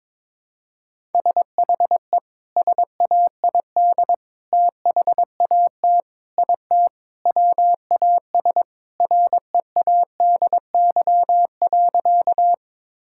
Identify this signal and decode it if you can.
Morse code, heard as a keyed tone, about 22 wpm, 700 Hz. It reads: SHE SAID THAT IT WAS READY.
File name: practice / 22wpm / Sentences with 3-Word Phrases - 22wpm